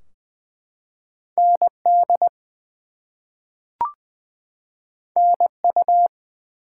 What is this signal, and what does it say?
Morse code, heard as a keyed tone, about 20 wpm, 700 Hz. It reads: ND NU
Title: US States and Canadian Provinces 20wpm